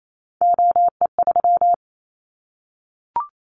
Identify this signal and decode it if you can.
Morse code, heard as a keyed tone, about 28 wpm, 700 Hz. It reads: OE3
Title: Callsign Prefixes for Top 100 DXCC - 28wpm